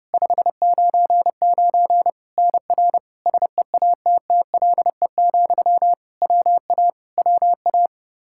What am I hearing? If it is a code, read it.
Morse code, heard as a keyed tone, about 32 wpm, 700 Hz. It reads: 599 NR SEATTLE, WA WA
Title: Practice QSO - Exchange 1 - Beginning - 30wpm